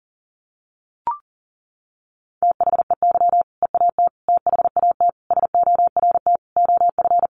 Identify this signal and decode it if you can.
Morse code, heard as a keyed tone, about 40 wpm, 700 Hz. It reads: THEY EAT THAT SORT OF
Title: Sentences from Top 1400 Words 40wpm